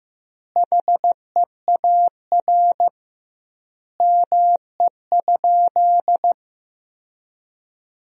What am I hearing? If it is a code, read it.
Morse code, heard as a keyed tone, about 15 wpm, 700 Hz. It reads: HEAR ME?
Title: 2x Word-Spacing - Sentences from Top 300 Words - 15wpm